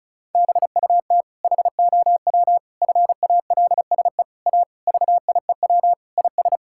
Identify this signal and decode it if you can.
Morse code, heard as a keyed tone, about 34 wpm, 700 Hz. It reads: BUT HOW FALSE A VIEW IS
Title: Sentences from Top 1600 Words 35wpm